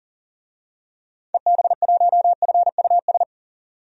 Morse code, heard as a keyed tone, about 40 wpm, 700 Hz. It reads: EB1FUS
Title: DXCC-100 Call Signs - 6 Characters - 40wpm